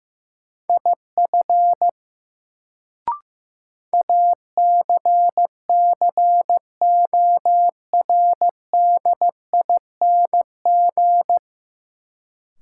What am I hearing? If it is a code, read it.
Morse code, heard as a keyed tone, about 15 wpm, 700 Hz. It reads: IF ACCORDING
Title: Sets of 2 Words - New Word Drill - 15wpm